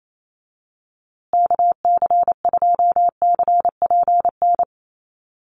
Morse code, heard as a keyed tone, about 28 wpm, 700 Hz. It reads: KC2CPN
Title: USA Call Signs Repeated 3x 28wpm